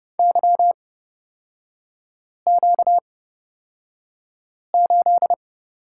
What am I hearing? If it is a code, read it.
Morse code, heard as a keyed tone, about 29 wpm, spaced as for 5 wpm, 690 Hz. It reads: YQ8
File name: Lesson 38e - Letter q - Three Characters 30wpm